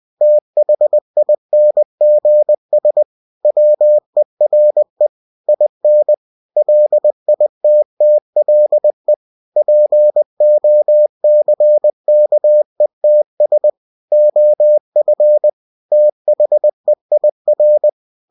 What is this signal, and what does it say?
Morse code, heard as a keyed tone, about 20 wpm, 600 Hz. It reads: THINGS WERE IN LITTLE POCKETS OF THEIR
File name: cw-books / winnie-the-pooh / winnie-20wpm-page-0109